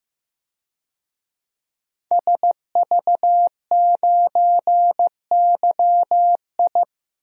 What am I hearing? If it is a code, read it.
Morse code, heard as a keyed tone, about 15 wpm, 700 Hz. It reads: SV9YI
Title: DXCC-100 Call Signs - 5 Characters - 15wpm